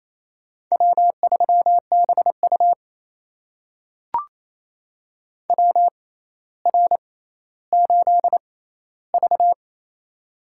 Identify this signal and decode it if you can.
Morse code, heard as a keyed tone, about 27 wpm, spaced as for 14 wpm, 700 Hz. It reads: W3BU WR8V
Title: USA Call Signs - 4 Characters - 6x Character Spacing - 28wpm